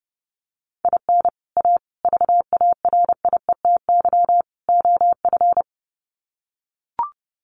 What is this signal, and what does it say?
Morse code, heard as a keyed tone, about 30 wpm, 700 Hz. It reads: IN A VARIETY OF
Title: Sets of 4 Words 30wpm